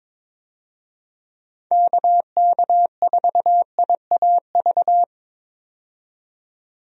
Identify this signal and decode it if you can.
Morse code, heard as a keyed tone, about 22 wpm, 700 Hz. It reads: KK4IAV
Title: USA Call Signs - 6 Characters - 22wpm